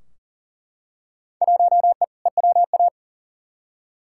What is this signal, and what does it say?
Morse code, heard as a keyed tone, about 40 wpm, 700 Hz. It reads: 1E EWA
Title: Field Day Exchanges - 40wpm